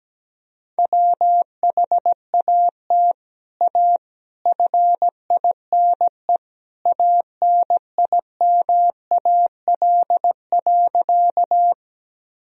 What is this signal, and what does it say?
Morse code, heard as a keyed tone, about 17 wpm, 700 Hz. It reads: WHAT A FINE ANIMAL.
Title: Sentences from Top 1000 Words - Review Part 1 - 17wpm